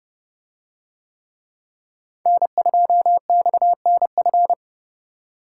Morse code, heard as a keyed tone, about 29 wpm, 700 Hz. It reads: N2XNF